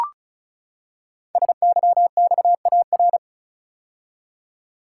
Morse code, heard as a keyed tone, about 36 wpm, 690 Hz. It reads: SYXAR
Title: Code Groups - 5 Letters - 35wpm